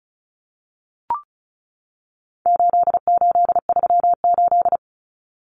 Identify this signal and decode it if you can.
Morse code, heard as a keyed tone, about 34 wpm, 700 Hz. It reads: 8838
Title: Numbers - 4-Digit 35wpm